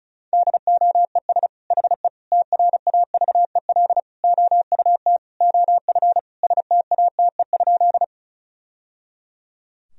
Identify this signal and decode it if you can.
Morse code, heard as a keyed tone, about 36 wpm, 700 Hz. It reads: DOES HE TRAVEL OUT OF STATE?
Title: Sentences from Top 1000 Words - Review Part 2 - 35wpm